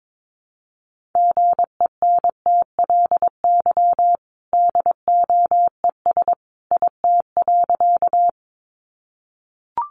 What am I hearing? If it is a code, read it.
Morse code, heard as a keyed tone, about 22 wpm, 700 Hz. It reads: GENTLY DOES IT.
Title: Sentences from Top 1500 Words 22wpm